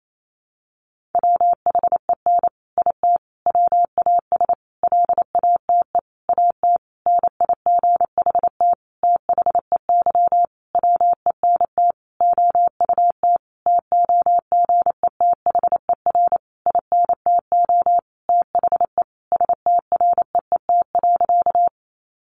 Morse code, heard as a keyed tone, about 28 wpm, 700 Hz. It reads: WHEN IT WAS LATE AT NIGHT THEY WENT OUT TOGETHER INTO THE STREET.
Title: Sentences from Top 600 Words 28wpm